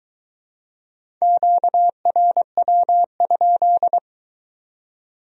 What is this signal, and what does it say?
Morse code, heard as a keyed tone, about 21 wpm, 700 Hz. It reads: QRW?